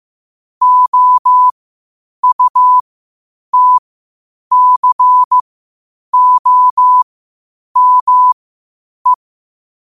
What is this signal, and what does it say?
Morse code, heard as a keyed tone, about 15 wpm, spaced as for 8 wpm, 1000 Hz. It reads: OUTCOME
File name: qotdfiles / 8wpm